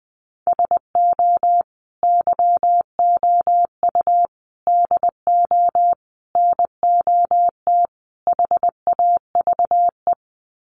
Morse code, heard as a keyed tone, about 20 wpm, 700 Hz. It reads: SO YOU DO NOT HAVE
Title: Sets of 5 Words Encore 20wpm